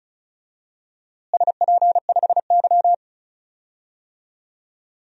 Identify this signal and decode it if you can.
Morse code, heard as a keyed tone, about 36 wpm, 700 Hz. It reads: SP5Y
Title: DXCC-100 Call Signs - 4 Characters - Repeated 2x - 35wpm